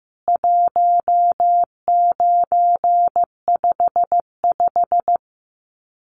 Morse code, heard as a keyed tone, about 15 wpm, 700 Hz. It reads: 1955
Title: CWT Exchange 15wpm